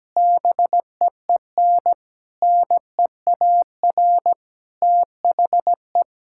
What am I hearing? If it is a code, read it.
Morse code, heard as a keyed tone, about 17 wpm, 700 Hz. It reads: BEEN NEAR THE
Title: Sentences from Top 1000 Words - Review Part 6 - 17wpm